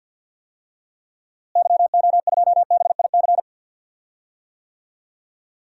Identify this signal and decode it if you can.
Morse code, heard as a keyed tone, about 51 wpm, 700 Hz. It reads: YO2BIC
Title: DXCC-100 Call Signs - 6 Characters - Repeated 3x - 50wpm